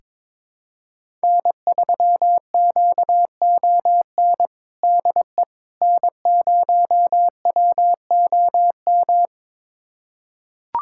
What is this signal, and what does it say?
Morse code, heard as a keyed tone, about 22 wpm, 700 Hz. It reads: N3QON DE N0WOM
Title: USA Callsign DE Callsign - 5 Characters - 22wpm